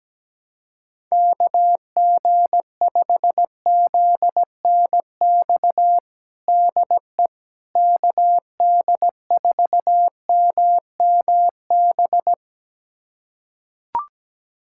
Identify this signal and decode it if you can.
Morse code, heard as a keyed tone, about 17 wpm, 700 Hz. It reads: KG5ZNX DE KD4MMB
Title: USA Callsign DE Callsign 17wpm